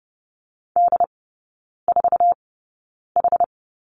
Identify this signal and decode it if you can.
Morse code, heard as a keyed tone, about 30 wpm, spaced as for 9 wpm, 700 Hz. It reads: D4H
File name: Lesson 18e - Letter m - Three Characters 30wpm